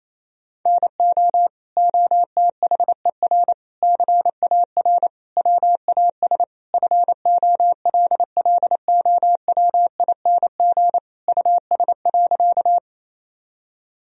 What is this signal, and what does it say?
Morse code, heard as a keyed tone, about 28 wpm, 700 Hz. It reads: NO OTHER CAR WAS FOLLOWING US.
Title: Sentences from Top 1000 Words - Review Part 4 - 28wpm